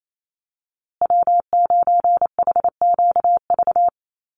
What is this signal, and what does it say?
Morse code, heard as a keyed tone, about 28 wpm, 700 Hz. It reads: W9HQV
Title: USA Call Signs Repeated 2x 28wpm